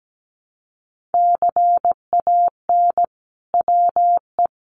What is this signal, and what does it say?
Morse code, heard as a keyed tone, about 17 wpm, 700 Hz. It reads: CAN WE
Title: Sentences from Top 1400 Words 17wpm